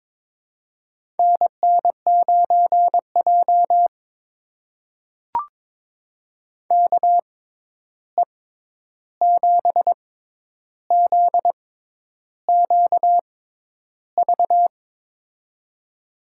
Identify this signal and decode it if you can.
Morse code, heard as a keyed tone, about 21 wpm, spaced as for 10 wpm, 700 Hz. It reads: NN9J KE7ZQV
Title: USA Call Signs - 6x Character Spacing - 22wpm